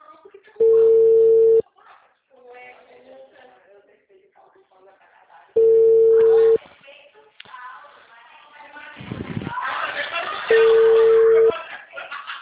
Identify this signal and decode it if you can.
Ringback tone (China, 450 Hz), with speech.